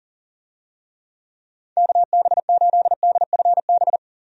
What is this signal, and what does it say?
Morse code, heard as a keyed tone, about 40 wpm, 700 Hz. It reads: KB8DFB
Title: USA Call Signs - 6 Characters - 40wpm